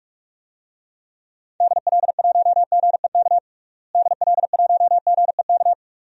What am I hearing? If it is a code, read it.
Morse code, heard as a keyed tone, about 44 wpm, 700 Hz. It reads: DL1GEK DL1GEK
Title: DXCC-100 Call Signs - 6 Characters - Repeated 2x - 45wpm